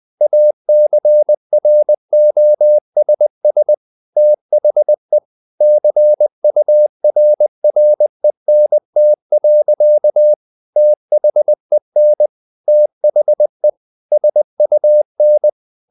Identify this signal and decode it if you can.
Morse code, heard as a keyed tone, about 20 wpm, 600 Hz. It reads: ACROSS THE CURRENT. THEN THE SUN
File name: the old man and the sea / the old man and the sea part 1 20wpm